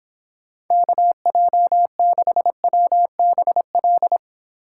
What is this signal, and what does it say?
Morse code, heard as a keyed tone, about 26 wpm, 700 Hz. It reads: KJ6WBL